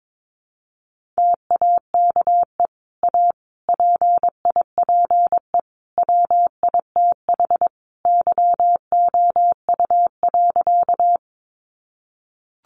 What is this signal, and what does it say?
Morse code, heard as a keyed tone, about 22 wpm, 700 Hz. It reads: TAKE A PIPE WITH YOU.
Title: Sentences from Top 1500 Words 22wpm